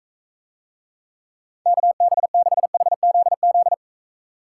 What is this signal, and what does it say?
Morse code, heard as a keyed tone, about 41 wpm, 700 Hz. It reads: KB6HZZ